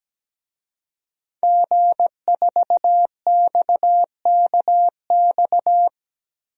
Morse code, heard as a keyed tone, about 17 wpm, 700 Hz. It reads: G4XKX
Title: DXCC-100 Call Signs - 5 Characters - 17wpm